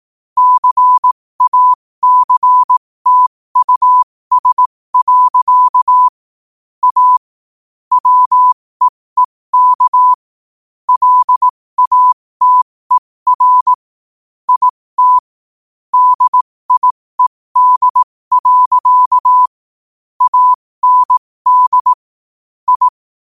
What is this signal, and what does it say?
Morse code, heard as a keyed tone, about 18 wpm, spaced as for 15 wpm, 1000 Hz. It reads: CACTUS. A WEEK LATER IT DIED. AND I
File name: qotdfiles / 15wpm